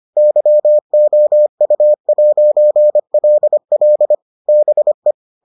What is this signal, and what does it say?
Morse code, heard as a keyed tone, about 25 wpm, 600 Hz. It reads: YOU'LL BE